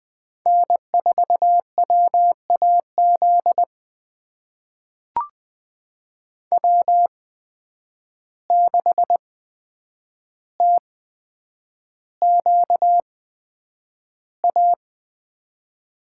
Morse code, heard as a keyed tone, about 20 wpm, spaced as for 9 wpm, 700 Hz. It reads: N4WAZ W6TQA
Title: USA Call Signs - 5 Characters - 8x Character Spacing - 20wpm